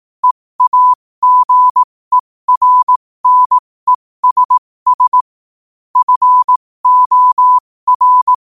Morse code, heard as a keyed tone, about 18 wpm, spaced as for 15 wpm, 1000 Hz. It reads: EAGERNESS FOR
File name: qotdfiles / 15wpm